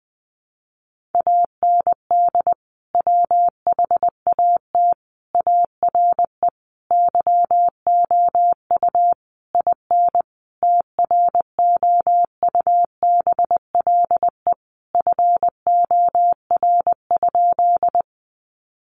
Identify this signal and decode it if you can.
Morse code, heard as a keyed tone, about 20 wpm, 700 Hz. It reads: AND WHAT ARE YOU IN TROUBLE FOR?
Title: Sentences from Top 1200 Words 20wpm